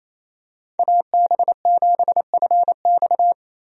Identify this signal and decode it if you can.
Morse code, heard as a keyed tone, about 28 wpm, 700 Hz. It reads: AB7FX
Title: USA Call Signs - 5 Characters - Repeated 3x - 28wpm